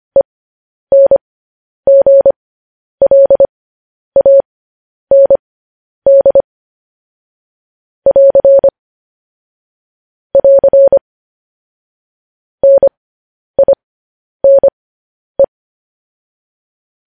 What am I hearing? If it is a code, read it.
Morse code, heard as a keyed tone, about 25 wpm, spaced as for 10 wpm, 550 Hz. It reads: ENGLAND <AR> <AR> NINE
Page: news-headlines-in-morse-code-at-10-wpm